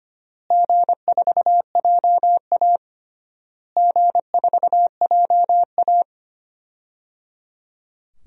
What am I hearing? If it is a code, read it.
Morse code, heard as a keyed tone, about 25 wpm, 700 Hz. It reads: G4JA G4JA
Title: DXCC-100 Call Signs - 4 Characters - Repeated 3x - 25wpm